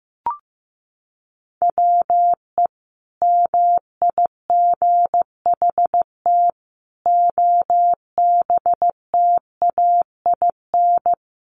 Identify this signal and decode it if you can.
Morse code, heard as a keyed tone, about 15 wpm, 700 Hz. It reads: WE MIGHT OBTAIN
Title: Sentences from Top 1300 Words 15wpm